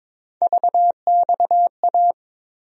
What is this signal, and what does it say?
Morse code, heard as a keyed tone, about 22 wpm, 700 Hz. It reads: VXA
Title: USA Call Sign Postfixes - 3 Letters 22wpm